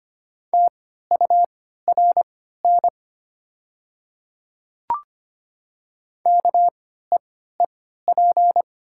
Morse code, heard as a keyed tone, about 25 wpm, spaced as for 14 wpm, 700 Hz. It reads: TURN KEEP
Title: 3x Character Spacing - Top 300 Words - 25wpm